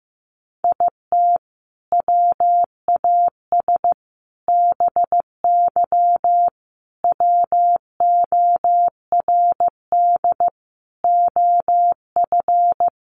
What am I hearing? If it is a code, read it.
Morse code, heard as a keyed tone, about 14 wpm, 700 Hz. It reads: IT WAS BY WORD OF
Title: Sentences from Top 1500 Words 15wpm